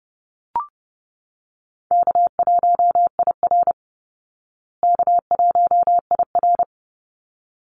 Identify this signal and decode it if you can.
Morse code, heard as a keyed tone, about 30 wpm, 700 Hz. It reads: K1IR K1IR
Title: Call Signs Repeated 3x - 30wpm